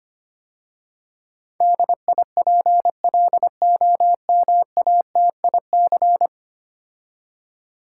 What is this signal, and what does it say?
Morse code, heard as a keyed tone, about 25 wpm, 700 Hz. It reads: DIPLOMATIC
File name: 10 Letter Words - First 3 Letters Given - 25wpm